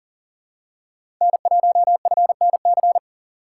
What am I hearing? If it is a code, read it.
Morse code, heard as a keyed tone, about 40 wpm, 700 Hz. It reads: N1FNC
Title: USA Call Signs - 5 Characters - 6x Character Spacing - 40wpm